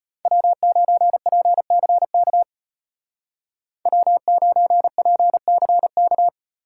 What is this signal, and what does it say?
Morse code, heard as a keyed tone, about 32 wpm, 700 Hz. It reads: W9PCK W9PCK